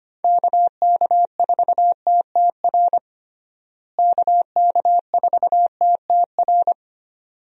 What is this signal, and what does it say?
Morse code, heard as a keyed tone, about 25 wpm, 700 Hz. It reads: KK4TTR KK4TTR
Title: USA Call Signs - 6 Characters - Repeated 2x - 25wpm